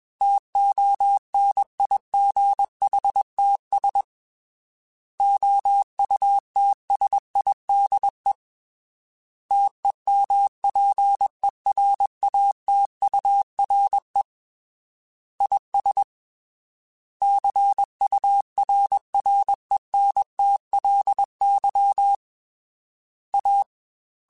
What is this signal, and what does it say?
Morse code, heard as a keyed tone, about 21 wpm, 800 Hz. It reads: TONIGHTS OUTSIDE TEMPERATURE IS CURRENTLY A